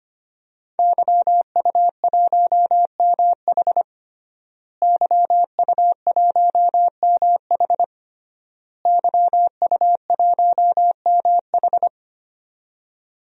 Morse code, heard as a keyed tone, about 25 wpm, 700 Hz. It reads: YU1MH YU1MH YU1MH
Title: DXCC-100 Call Signs - 5 Characters - Repeated 3x - 25wpm